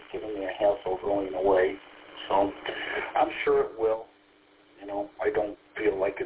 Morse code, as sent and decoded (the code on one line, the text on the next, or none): none